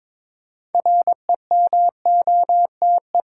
Morse code, heard as a keyed tone, about 21 wpm, 690 Hz. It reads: REMOTE
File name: Difficult Words with Hints - 22wpm